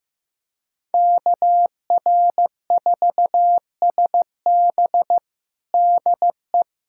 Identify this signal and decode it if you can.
Morse code, heard as a keyed tone, about 15 wpm, 700 Hz. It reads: KR4SB DE
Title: USA Callsign DE Callsign 15wpm